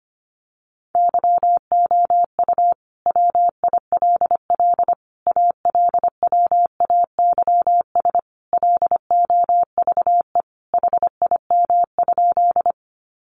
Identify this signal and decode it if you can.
Morse code, heard as a keyed tone, about 25 wpm, 700 Hz. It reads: YOU WILL ALWAYS LOVE HIM?
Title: Sentences from Top 400 Words 25wpm